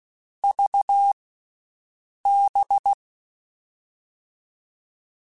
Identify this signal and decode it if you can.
Morse code, heard as a keyed tone, about 16 wpm, 800 Hz. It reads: V B
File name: Groups@10